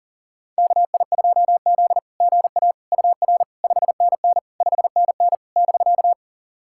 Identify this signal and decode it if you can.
Morse code, heard as a keyed tone, about 37 wpm, 700 Hz. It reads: KI2Z GA UR 5NN 5NN <BK>
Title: SOTA QSO Practice 40wpm